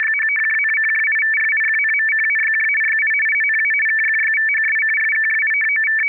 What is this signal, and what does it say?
Two-tone teleprinter signal, about 500 Hz shift.